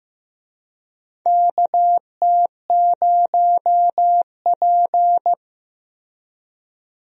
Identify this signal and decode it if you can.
Morse code, heard as a keyed tone, about 16 wpm, 700 Hz. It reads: KT0P
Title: USA Call Signs - 4 Characters - Repeated 3x - 15wpm